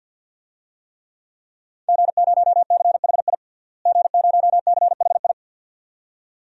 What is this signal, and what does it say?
Morse code, heard as a keyed tone, about 53 wpm, 700 Hz. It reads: G0CHI G0CHI
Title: DXCC-100 Call Signs - 5 Characters - Repeated 2x - 50wpm